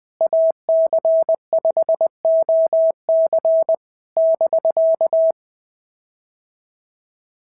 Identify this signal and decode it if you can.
Morse code, heard as a keyed tone, about 21 wpm, 650 Hz. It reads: AC5OC <BK>
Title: POTA QSO Practice - 20wpm